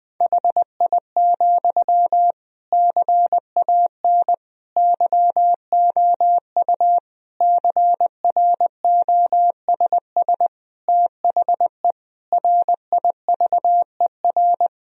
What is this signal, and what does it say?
Morse code, heard as a keyed tone, about 20 wpm, 700 Hz. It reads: HI, CAN YOU CROSS THE RIVER
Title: Sentences from Top 1000 Words - Review Part 3 - 20wpm